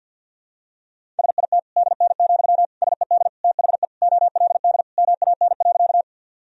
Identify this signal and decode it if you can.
Morse code, heard as a keyed tone, about 52 wpm, 700 Hz. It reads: SIT DN, SED THE OLD MAN.
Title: Sentences from Top 500 Words and Abbreviations 50wpm